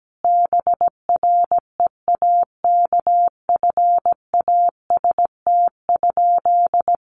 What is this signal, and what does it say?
Morse code, heard as a keyed tone, about 17 wpm, 700 Hz. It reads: BREAKFAST?
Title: Sentences from Top 1500 Words 17wpm